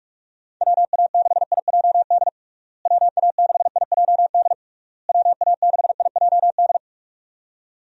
Morse code, heard as a keyed tone, about 42 wpm, 700 Hz. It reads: WA6IJD WA6IJD WA6IJD
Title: USA Call Signs - 6 Characters - Repeated 3x - 45wpm